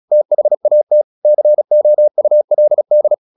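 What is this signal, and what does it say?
Morse code, heard as a keyed tone, about 36 wpm, 600 Hz. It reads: THAT COULD
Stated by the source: Morse practice at 35 wpm from CW med Gnister, sending Ken Follett's novel Never in English.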